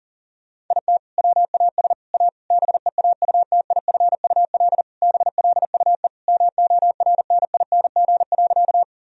Morse code, heard as a keyed tone, about 37 wpm, 700 Hz. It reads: IT WAS A BEAUTIFUL BLUE MORNING.
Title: Sentences from Top 1000 Words - Review Part 3 - 40wpm